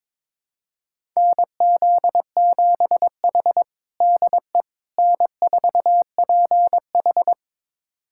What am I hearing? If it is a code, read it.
Morse code, heard as a keyed tone, about 22 wpm, 700 Hz. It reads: NZ7H DE N4PH
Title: USA Callsign DE Callsign - 4 Characters - 22wpm